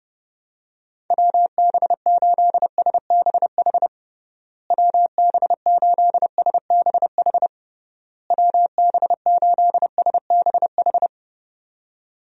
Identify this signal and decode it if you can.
Morse code, heard as a keyed tone, about 30 wpm, 700 Hz. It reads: WB8SBH WB8SBH WB8SBH
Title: USA Call Signs - 6 Characters - Repeated 3x - 30wpm